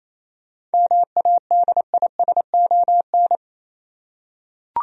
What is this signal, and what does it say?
Morse code, heard as a keyed tone, about 28 wpm, 700 Hz. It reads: MADISON
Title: USA Common Names - 28wpm